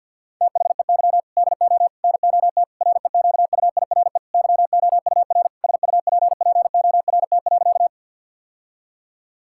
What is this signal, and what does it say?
Morse code, heard as a keyed tone, about 46 wpm, 700 Hz. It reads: THEY DO NOT REQUIRE YOUR SUPPORT.
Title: Sentences from Top 1000 Words - Review Part 6 - 50wpm